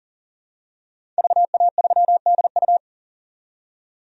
Morse code, heard as a keyed tone, about 40 wpm, 700 Hz. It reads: VA3DU
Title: DXCC-100 Call Signs - 5 Characters - 40wpm